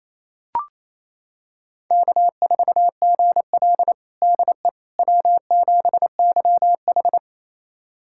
Morse code, heard as a keyed tone, about 28 wpm, 700 Hz. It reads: K4GL DE W7YH
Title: USA Callsign DE Callsign - 4 Characters - 28wpm